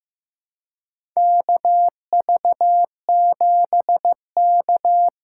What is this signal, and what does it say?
Morse code, heard as a keyed tone, about 15 wpm, 700 Hz. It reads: KV7K
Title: USA Call Signs - 2x Character Spacing - 15wpm